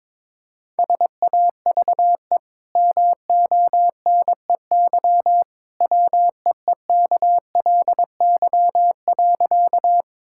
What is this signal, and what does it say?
Morse code, heard as a keyed tone, about 22 wpm, 700 Hz. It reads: SAVE MONEY WEEKLY.
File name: practice / 22wpm / Easy Sentences with Hints - 22wpm